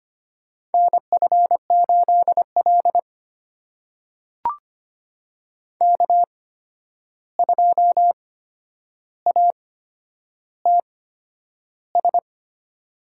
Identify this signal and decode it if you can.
Morse code, heard as a keyed tone, about 25 wpm, spaced as for 10 wpm, 700 Hz. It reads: NF8L K2ATS
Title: USA Call Signs - 8x Character Spacing - 25wpm